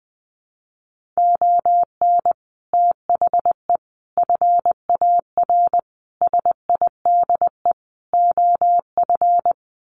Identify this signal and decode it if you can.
Morse code, heard as a keyed tone, about 20 wpm, 700 Hz. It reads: ON THE FAR SIDE OF
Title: Sets of 5 Words 20wpm